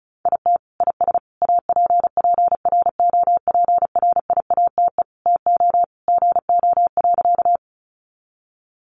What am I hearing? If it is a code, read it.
Morse code, heard as a keyed tone, about 34 wpm, 700 Hz. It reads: IT IS APPROPRIATE TO GO.
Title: Sentences from Top 900 Words 35wpm